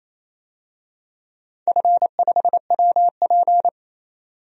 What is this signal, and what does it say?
Morse code, heard as a keyed tone, about 28 wpm, 700 Hz. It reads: F5WP
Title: DXCC-100 Call Signs - 4 Characters - 28wpm